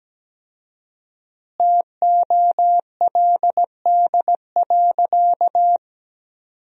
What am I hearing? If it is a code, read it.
Morse code, heard as a keyed tone, about 17 wpm, 700 Hz. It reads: TOLD.
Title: 4x Word-Spacing - Sentences from Top 400 Words - 17wpm